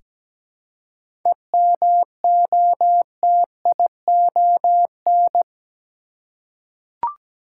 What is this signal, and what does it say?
Morse code, heard as a keyed tone, about 17 wpm, 700 Hz. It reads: EMOTION
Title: Seven Letter Words - 17wpm